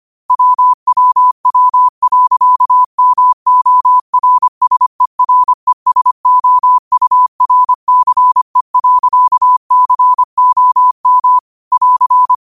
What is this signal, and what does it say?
Morse code, heard as a keyed tone, about 25 wpm, 1000 Hz. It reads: WWW.MORSERESOURCE.COM <AR>
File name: qotdfiles / 25wpm